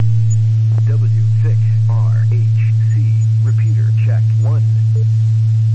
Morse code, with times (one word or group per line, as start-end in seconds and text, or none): none